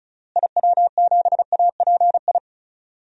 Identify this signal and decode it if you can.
Morse code, heard as a keyed tone, about 34 wpm, 700 Hz. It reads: IW7API
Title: DXCC-100 Call Signs - 6 Characters - Repeated 3x - 35wpm